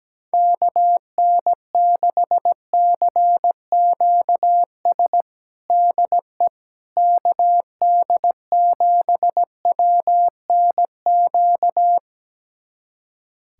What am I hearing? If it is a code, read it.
Morse code, heard as a keyed tone, about 17 wpm, 700 Hz. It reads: KN6CQS DE KD7WNQ
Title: USA Callsign DE Callsign - 6 Characters - 17wpm